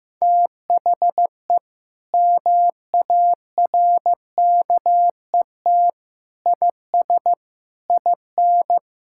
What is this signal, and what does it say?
Morse code, heard as a keyed tone, about 15 wpm, 700 Hz. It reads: THE MARKET IS IN
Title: Sentences from Top 1000 Words - Review Part 6 - 15wpm